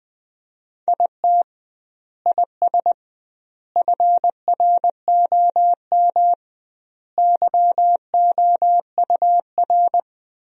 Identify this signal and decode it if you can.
Morse code, heard as a keyed tone, about 21 wpm, 700 Hz. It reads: IT IS FROM YOUR
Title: 2x Word-Spacing - Sentences from Top 300 Words - 20wpm